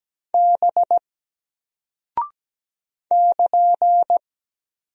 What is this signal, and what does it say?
Morse code, heard as a keyed tone, about 17 wpm, 700 Hz. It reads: B <KN>
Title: Single Character (Letter:Number:Punctuation:Prosign) Alt 17wpm